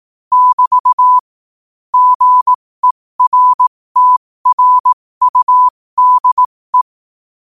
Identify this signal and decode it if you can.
Morse code, heard as a keyed tone, about 18 wpm, spaced as for 15 wpm, 1000 Hz. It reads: <BT> GERTRUDE
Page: qotd-at-15-wpm